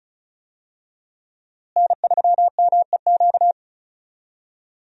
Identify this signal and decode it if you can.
Morse code, heard as a keyed tone, about 34 wpm, 700 Hz. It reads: N3MEQ
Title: USA Call Signs - 5 Characters - Repeated 2x - 35wpm